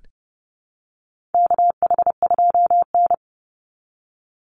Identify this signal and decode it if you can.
Morse code, heard as a keyed tone, about 30 wpm, 700 Hz. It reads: KH2N
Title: Call Signs Repeated 3x - 30wpm